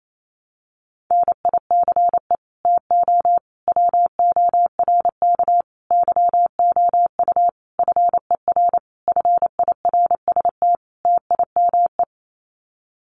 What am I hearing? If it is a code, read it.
Morse code, heard as a keyed tone, about 28 wpm, 700 Hz. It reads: NICE TO WORK YOU FER FIRST TIME
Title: Common QSO Phrases 28wpm